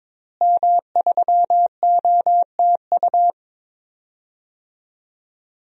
Morse code, heard as a keyed tone, about 22 wpm, 700 Hz. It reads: M3OTU
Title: DXCC-100 Call Signs - 5 Characters - Repeated 3x - 22wpm